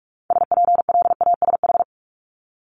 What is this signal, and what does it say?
Morse code, heard as a keyed tone, about 46 wpm, 700 Hz. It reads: SPLASH
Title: Multi-letter Phonemes Repeated 3x 45wpm